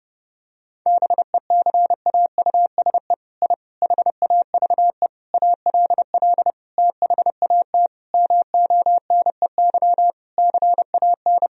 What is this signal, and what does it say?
Morse code, heard as a keyed tone, about 30 wpm, 700 Hz. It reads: BECAUSE I HAVE ALL THAT MONEY CAN
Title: Sentences from Top 1000 Words - Review Part 4 - 30wpm